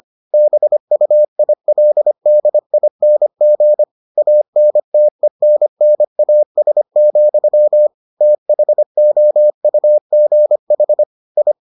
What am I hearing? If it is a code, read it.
Morse code, heard as a keyed tone, about 25 wpm, 600 Hz. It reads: BUILDING ANTENNAS, THOUGH I